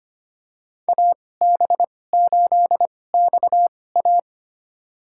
Morse code, heard as a keyed tone, about 25 wpm, spaced as for 18 wpm, 700 Hz. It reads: AB8XA
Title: USA Call Signs - 5 Characters - 2x Character Spacing - 25wpm